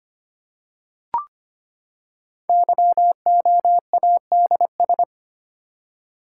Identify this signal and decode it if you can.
Morse code, heard as a keyed tone, about 25 wpm, 700 Hz. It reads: YOADS
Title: Code Groups - 5 Letter:Numbers - 25wpm